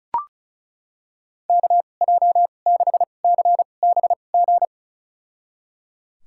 Morse code, heard as a keyed tone, about 36 wpm, spaced as for 25 wpm, 700 Hz. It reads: KJ6CBG
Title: USA Call Signs - 2x Character Spacing - 35wpm